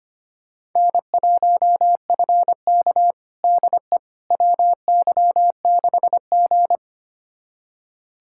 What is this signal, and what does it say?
Morse code, heard as a keyed tone, about 25 wpm, 700 Hz. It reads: N1FK DE WY6G
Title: USA Callsign DE Callsign - 4 Characters - 25wpm